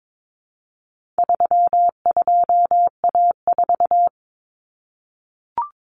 Morse code, heard as a keyed tone, about 22 wpm, 700 Hz. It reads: 32A4
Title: Common Cut Numbers - 4-Digit 22wpm